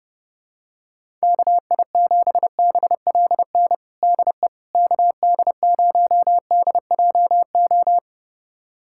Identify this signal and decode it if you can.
Morse code, heard as a keyed tone, about 32 wpm, 700 Hz. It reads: KI7BLN DE KD0DJO
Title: USA Callsign DE Callsign 30wpm